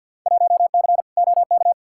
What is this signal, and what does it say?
Morse code, heard as a keyed tone, about 51 wpm, 700 Hz. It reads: 1C OK
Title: Field Day Exchanges Encore - 50wpm